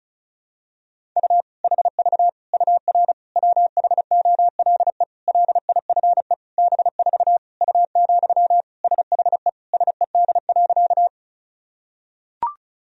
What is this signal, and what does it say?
Morse code, heard as a keyed tone, about 36 wpm, 700 Hz. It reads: U HV UR WHOLE LIFE B4 U, SHE SED.
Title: Sentences from Top 400 Words and Abbreviations 35wpm